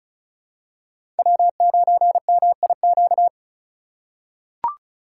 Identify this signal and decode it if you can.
Morse code, heard as a keyed tone, about 36 wpm, 700 Hz. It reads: W9MIQ
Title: USA Call Signs - 5 Characters - 4x Character Spacing - 35wpm